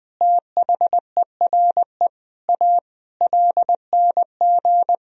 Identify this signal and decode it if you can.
Morse code, heard as a keyed tone, about 20 wpm, 700 Hz. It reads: THERE A LNG
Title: Sentences from Top 400 Words and Abbreviations 20wpm